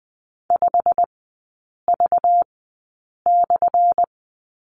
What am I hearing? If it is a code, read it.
Morse code, heard as a keyed tone, about 20 wpm, spaced as for 8 wpm, 700 Hz. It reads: 5V/
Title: 212 Lesson 34e - Number 8 - Three Characters 20wpm